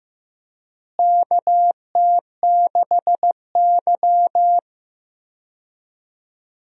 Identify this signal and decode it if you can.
Morse code, heard as a keyed tone, about 16 wpm, 700 Hz. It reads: KT6Y
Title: USA Call Signs - 4 Characters - Repeated 3x - 15wpm